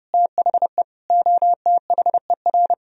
Morse code, heard as a keyed tone, about 40 wpm, spaced as for 35 wpm, 700 Hz. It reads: THE OTHER
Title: Speed-Racing - Sets of 5 Words 30wpm